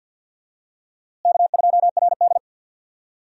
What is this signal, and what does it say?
Morse code, heard as a keyed tone, about 52 wpm, 700 Hz. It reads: K2RD
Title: USA Call Signs - 4 Characters - Repeated 2x - 50wpm